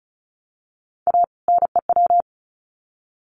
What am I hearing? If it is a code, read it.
Morse code, heard as a keyed tone, about 34 wpm, 700 Hz. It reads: A NEW
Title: Sets of 2 Words 35wpm